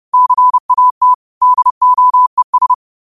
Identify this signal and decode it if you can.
Morse code, heard as a keyed tone, about 30 wpm, 1000 Hz. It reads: CAT DOES